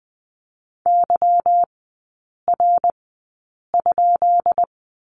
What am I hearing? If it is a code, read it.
Morse code, heard as a keyed tone, about 20 wpm, spaced as for 8 wpm, 700 Hz. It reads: YR?
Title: 178 Lesson 29e - Slash - Three Characters 20wpm